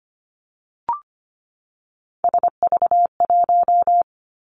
Morse code, heard as a keyed tone, about 25 wpm, 700 Hz. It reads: SV1
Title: Callsign Prefixes for Top 50 DXCC - 25wpm